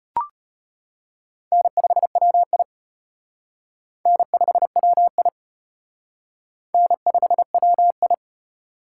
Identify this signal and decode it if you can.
Morse code, heard as a keyed tone, about 31 wpm, 700 Hz. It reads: N5WI N5WI N5WI